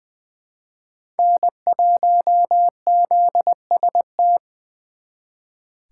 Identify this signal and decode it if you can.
Morse code, heard as a keyed tone, about 23 wpm, 700 Hz. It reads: N1ZST
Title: Speed Racing - USA 1x3 Call Signs 20wpm